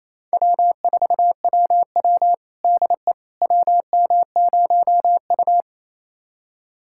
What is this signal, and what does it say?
Morse code, heard as a keyed tone, about 28 wpm, 700 Hz. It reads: W4WW DE WM0U
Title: USA Callsign DE Callsign - 4 Characters - 28wpm